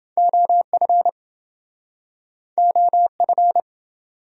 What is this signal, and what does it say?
Morse code, heard as a keyed tone, about 28 wpm, 700 Hz. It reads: OF OF